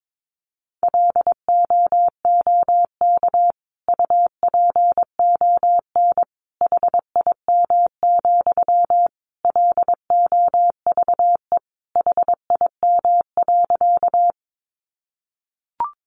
Morse code, heard as a keyed tone, about 22 wpm, 700 Hz. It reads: LOOK UPON HIM, LOVE HIM.
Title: Sentences from Top 1100 Words 22wpm